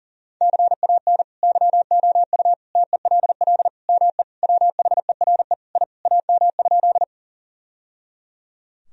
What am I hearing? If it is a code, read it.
Morse code, heard as a keyed tone, about 37 wpm, 700 Hz. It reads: CAN YOU TELL ME WHERE I AM?
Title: Sentences from Top 1000 Words - Review Part 2 - 40wpm